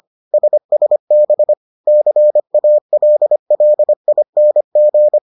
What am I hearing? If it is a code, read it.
Morse code, heard as a keyed tone, about 25 wpm, 600 Hz. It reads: SSB CALLING